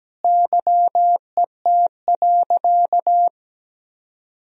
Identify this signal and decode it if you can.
Morse code, heard as a keyed tone, about 17 wpm, 700 Hz. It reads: YET.
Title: Sentences from Top 1000 Words - Review Part 2 - 17wpm